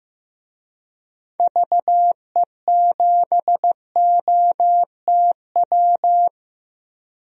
Morse code, heard as a keyed tone, about 16 wpm, 700 Hz. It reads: VE7OTW
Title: DXCC-100 Call Signs - 6 Characters - Repeated 3x - 15wpm